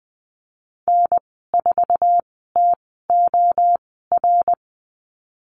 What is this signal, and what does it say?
Morse code, heard as a keyed tone, about 20 wpm, spaced as for 14 wpm, 700 Hz. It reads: N4TOR
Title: 055 Lesson 10g - Letter r - Call Signs 20wpm